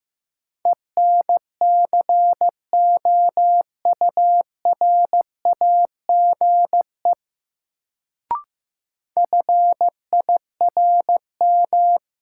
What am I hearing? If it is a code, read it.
Morse code, heard as a keyed tone, about 15 wpm, 700 Hz. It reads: ENCOURAGE FIRM
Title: Sentences from Top 1400 Words - New Word Drill - 15wpm